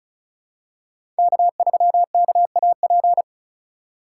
Morse code, heard as a keyed tone, about 36 wpm, 700 Hz. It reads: K3KAP
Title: USA Call Signs - 5 Characters - 3x Character Spacing - 35wpm